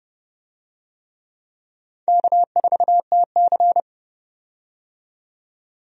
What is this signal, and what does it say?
Morse code, heard as a keyed tone, about 30 wpm, 700 Hz. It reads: K4TC
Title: USA Call Signs - 4 Characters - 30wpm